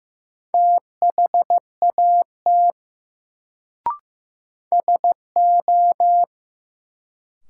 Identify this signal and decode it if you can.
Morse code, heard as a keyed tone, about 15 wpm, 700 Hz. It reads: THAT SO
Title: Sentences from Top 100 Words - New Word Drill - 15wpm